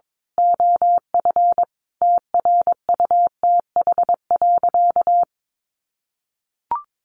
Morse code, heard as a keyed tone, about 22 wpm, 700 Hz. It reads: OF TRUTH.
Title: Sentences from Top 1100 Words 22wpm